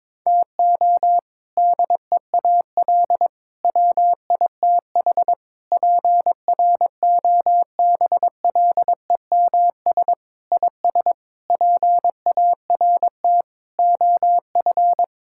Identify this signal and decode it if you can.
Morse code, heard as a keyed tone, about 22 wpm, 700 Hz. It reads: TO DEAL WITH PROBLEMS IS PART OF
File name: practice / 22wpm / Sentences with 3-Word Phrases - 22wpm